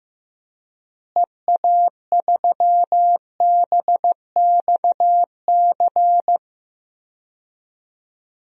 Morse code, heard as a keyed tone, about 15 wpm, 700 Hz. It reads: EA3BXC
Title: DXCC-100 Call Signs - 6 Characters - Repeated 3x - 15wpm